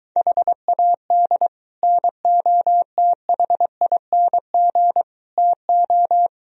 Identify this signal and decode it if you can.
Morse code, heard as a keyed tone, about 23 wpm, 700 Hz. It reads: HAD NOTHING TO